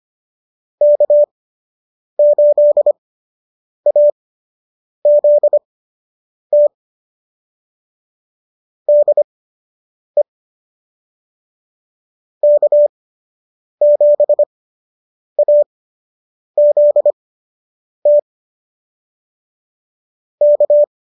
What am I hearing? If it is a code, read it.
Morse code, heard as a keyed tone, about 25 wpm, spaced as for 8 wpm, 590 Hz. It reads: K8AZT DE K7AZT K
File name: qso006 08